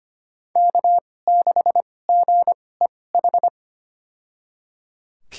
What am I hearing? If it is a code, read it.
Morse code, heard as a keyed tone, about 25 wpm, spaced as for 18 wpm, 700 Hz. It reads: K6GEH
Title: USA Call Signs - 2x Character Spacing - 25wpm